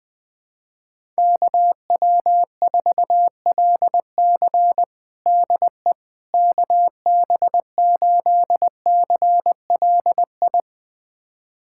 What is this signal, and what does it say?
Morse code, heard as a keyed tone, about 20 wpm, 700 Hz. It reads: KW4LC DE KB8CLI
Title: USA Callsign DE Callsign 20wpm